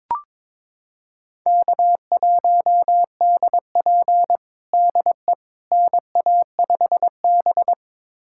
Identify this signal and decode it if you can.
Morse code, heard as a keyed tone, about 22 wpm, 700 Hz. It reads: K1DP DE NA5B
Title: USA Callsign DE Callsign - 4 Characters - 22wpm